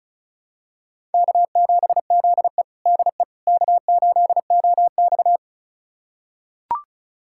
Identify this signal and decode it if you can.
Morse code, heard as a keyed tone, about 34 wpm, 700 Hz. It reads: K7ZE DE K8OX
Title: USA Callsign DE Callsign - 4 Characters - 35wpm